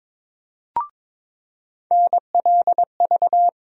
Morse code, heard as a keyed tone, about 21 wpm, 700 Hz. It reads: NLV
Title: USA Call Sign Postfixes - 3 Letters 22wpm